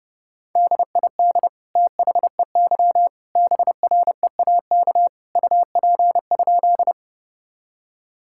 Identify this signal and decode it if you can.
Morse code, heard as a keyed tone, about 32 wpm, 700 Hz. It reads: DID THEY BREAK UP?
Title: Sentences from Top 1000 Words - Review Part 6 - 30wpm